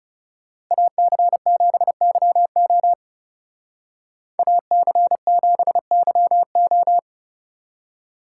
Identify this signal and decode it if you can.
Morse code, heard as a keyed tone, about 35 wpm, 700 Hz. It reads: AC7YO AC7YO
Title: Speed Racing - USA 2x2 Call Signs 30wpm